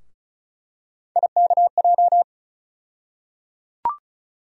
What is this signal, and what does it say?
Morse code, heard as a keyed tone, about 34 wpm, 700 Hz. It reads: IKJ
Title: USA Call Sign Postfixes - 3 Letters 35wpm